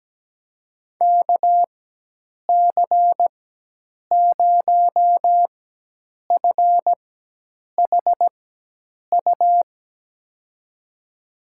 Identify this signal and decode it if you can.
Morse code, heard as a keyed tone, about 17 wpm, spaced as for 8 wpm, 700 Hz. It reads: KC0FHU
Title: USA Call Signs - 6 Characters - 4x Character Spacing - 17wpm